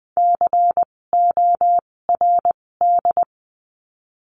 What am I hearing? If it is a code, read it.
Morse code, heard as a keyed tone, about 20 wpm, spaced as for 16 wpm, 700 Hz. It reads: CORD
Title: 256 Lesson 42a - Words 12-20wpm